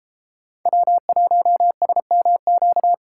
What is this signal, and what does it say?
Morse code, heard as a keyed tone, about 34 wpm, 700 Hz. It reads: W1SMQ
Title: Speed Racing - USA 1x3 Call Signs 28wpm